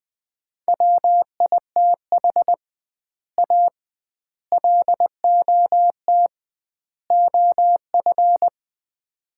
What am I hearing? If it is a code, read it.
Morse code, heard as a keyed tone, about 20 wpm, 700 Hz. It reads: WITH A LOT OF
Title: 2x Word-Spacing - Sets of 4 Words - 20wpm